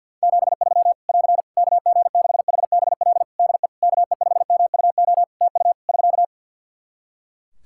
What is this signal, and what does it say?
Morse code, heard as a keyed tone, about 52 wpm, 700 Hz. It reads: 73 <AR> KG6HBL DE KE5MUO TU <SK>
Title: Practice QSO - Exchange 4 - Ending - 50wpm